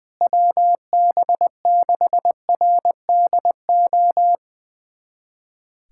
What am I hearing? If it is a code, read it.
Morse code, heard as a keyed tone, about 20 wpm, 700 Hz. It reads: WB6RDO
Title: USA Callsign DE Callsign 20wpm